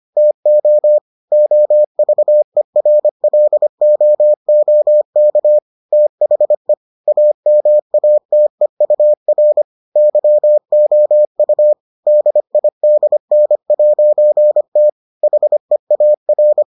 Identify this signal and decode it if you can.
Morse code, heard as a keyed tone, about 25 wpm, 600 Hz. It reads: TO OVERLOOK THE AMATEUR YOU DIDN'T HEAR